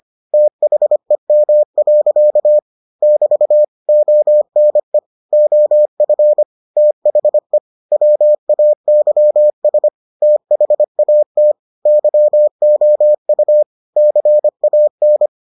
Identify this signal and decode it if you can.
Morse code, heard as a keyed tone, about 25 wpm, 600 Hz. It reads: THEM. <BT> ONE OF THE WAYS THAT YOU CAN